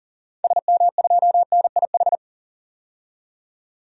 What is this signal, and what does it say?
Morse code, heard as a keyed tone, about 40 wpm, 700 Hz. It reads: SM2NIH
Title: DXCC-100 Call Signs - 6 Characters - 40wpm